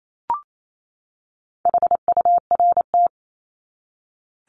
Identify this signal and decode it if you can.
Morse code, heard as a keyed tone, about 28 wpm, 700 Hz. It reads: HURT
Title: Top 1751-2000 Words 28wpm